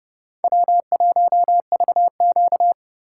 Morse code, heard as a keyed tone, about 33 wpm, 700 Hz. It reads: W1VQ